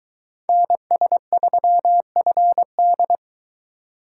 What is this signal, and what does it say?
Morse code, heard as a keyed tone, about 21 wpm, 700 Hz. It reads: NS3FD